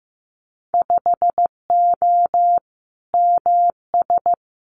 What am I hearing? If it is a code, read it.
Morse code, heard as a keyed tone, about 15 wpm, 700 Hz. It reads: 5O MS
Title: Winter Field Day Exchanges Encore 15wpm